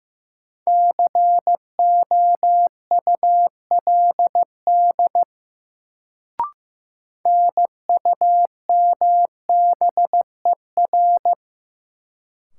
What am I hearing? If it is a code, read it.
Morse code, heard as a keyed tone, about 15 wpm, 700 Hz. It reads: COULD NUMBER
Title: Sets of 2 Words - New Word Drill - 15wpm